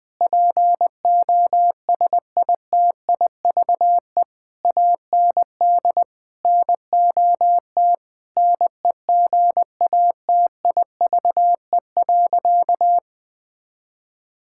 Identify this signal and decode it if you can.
Morse code, heard as a keyed tone, about 20 wpm, 700 Hz. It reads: POSITIVE AND NOT NEGATIVE.
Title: Sentences from Top 1000 Words - Review Part 6 - 20wpm